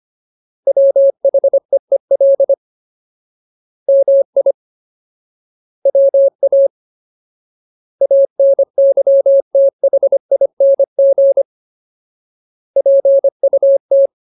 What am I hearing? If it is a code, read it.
Morse code, heard as a keyed tone, about 25 wpm, 550 Hz. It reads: WHEEL MI WA ANYTHING PUT
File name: Ultimate Word List / 25WPM Ultimate